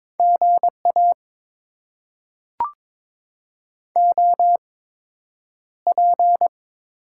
Morse code, heard as a keyed tone, about 22 wpm, spaced as for 9 wpm, 700 Hz. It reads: GA OP
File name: Top 90 QSO Elements - 8x Character Spacing - 22wpm